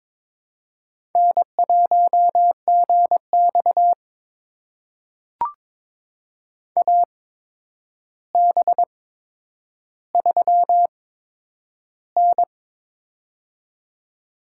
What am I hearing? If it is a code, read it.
Morse code, heard as a keyed tone, about 22 wpm, spaced as for 9 wpm, 700 Hz. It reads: N1GX AB3N
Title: USA Call Signs - 4 Characters - 8x Character Spacing - 22wpm